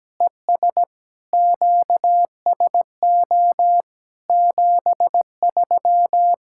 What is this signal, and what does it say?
Morse code, heard as a keyed tone, about 17 wpm, 700 Hz. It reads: ES QSO 73
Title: Practice QSO - Exchange 4 - Ending - 17wpm